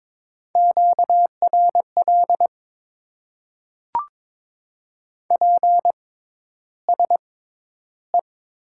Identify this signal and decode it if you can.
Morse code, heard as a keyed tone, about 21 wpm, spaced as for 11 wpm, 700 Hz. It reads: QRL PSE
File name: Top 90 QSO Elements - 6x Character Spacing - 22wpm